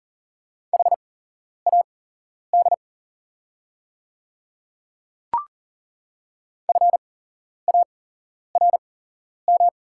Morse code, heard as a keyed tone, about 39 wpm, spaced as for 11 wpm, 690 Hz. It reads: HAD FARM
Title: 8x Character Spacing - Top 500 Words - 40wpm